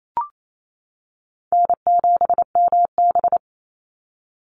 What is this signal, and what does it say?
Morse code, heard as a keyed tone, about 28 wpm, 700 Hz. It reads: N7MB
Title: Call Signs - 28wpm